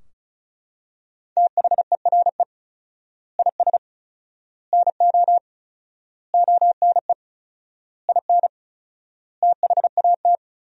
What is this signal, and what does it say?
Morse code, heard as a keyed tone, about 34 wpm, 700 Hz. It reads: THERE IS NO ONE IN THAT
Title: 4x Word-Spacing - Sentences from Top 500 Words - 35wpm